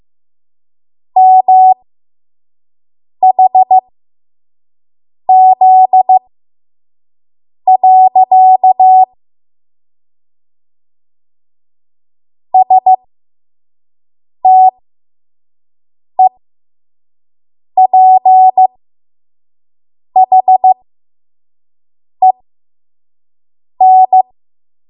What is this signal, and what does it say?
Morse code, heard as a keyed tone, about 15 wpm, spaced as for 5 wpm, 750 Hz. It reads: MHZ. STEPHEN
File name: Morse / 05WPM